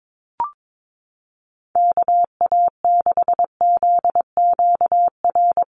Morse code, heard as a keyed tone, about 22 wpm, 700 Hz. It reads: KA6ZQR
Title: USA Call Signs Repeated 3x 22wpm